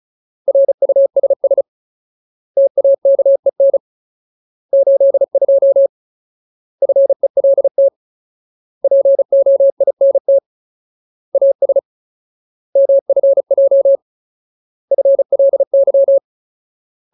Morse code, heard as a keyed tone, about 34 wpm, 550 Hz. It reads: RUSS TAKEN 82 FELT POINT AS MFJ FLY
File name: Ultimate Word List / 35WPM Ultimate